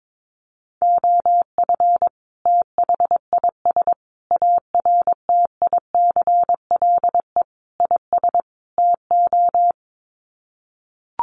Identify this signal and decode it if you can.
Morse code, heard as a keyed tone, about 22 wpm, 700 Hz. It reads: OF THIS ARTICLE IS TO
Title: Sets of 5 Words Encore2 22wpm